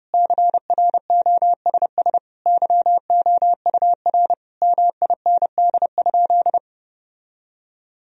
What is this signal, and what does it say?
Morse code, heard as a keyed tone, about 32 wpm, 700 Hz. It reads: CROSS YOUR MIND?
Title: Sentences from Top 1000 Words - Review Part 1 - 30wpm